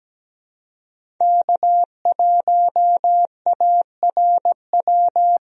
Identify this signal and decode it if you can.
Morse code, heard as a keyed tone, about 17 wpm, 700 Hz. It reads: K1ARW
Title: USA Call Signs - 5 Characters - Repeated 3x - 17wpm